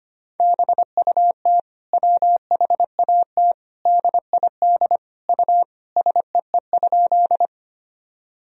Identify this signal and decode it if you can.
Morse code, heard as a keyed tone, about 25 wpm, 700 Hz. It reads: BUT WHAT DID U SEE?
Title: Sentences from Top 100 Words and Abbreviations 25wpm